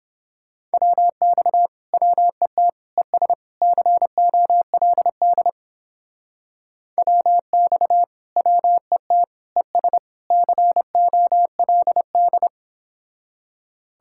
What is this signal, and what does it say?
Morse code, heard as a keyed tone, about 28 wpm, 700 Hz. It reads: WX WET ES COLD WX WET ES COLD